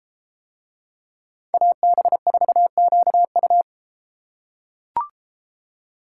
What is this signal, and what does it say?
Morse code, heard as a keyed tone, about 32 wpm, 700 Hz. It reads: AB4QU